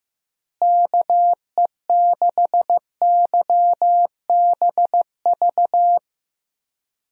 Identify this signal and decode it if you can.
Morse code, heard as a keyed tone, about 15 wpm, 700 Hz. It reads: KE6YBV
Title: USA Call Signs - 6 Characters - Repeated 3x - 15wpm